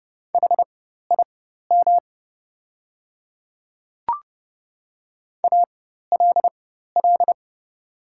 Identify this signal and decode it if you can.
Morse code, heard as a keyed tone, about 30 wpm, spaced as for 14 wpm, 700 Hz. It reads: HIM ALL
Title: 4x Character Spacing - Top 100 Words - 30wpm